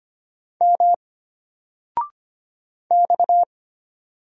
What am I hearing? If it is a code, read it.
Morse code, heard as a keyed tone, about 25 wpm, 700 Hz. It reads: M X
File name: Single-Letter-Number-25wpm